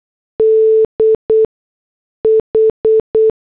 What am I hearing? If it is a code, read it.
Morse code, heard as a keyed tone, about 8 wpm, 440 Hz. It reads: D H